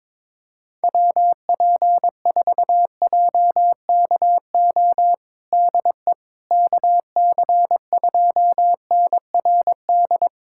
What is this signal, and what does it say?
Morse code, heard as a keyed tone, about 22 wpm, 700 Hz. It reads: WP4JKO DE KC2NRD
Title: USA Callsign DE Callsign 22wpm